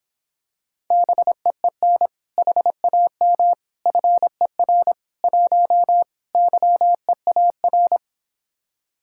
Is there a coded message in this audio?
Yes: Morse code from 0.9 to 8.0 s, heard as a keyed tone, about 26 wpm, 700 Hz. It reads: BEEN HAM FER 1 YEAR